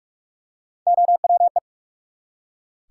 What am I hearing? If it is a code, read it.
Morse code, heard as a keyed tone, about 43 wpm, 700 Hz. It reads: OWE
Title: Three Letter Words - 45wpm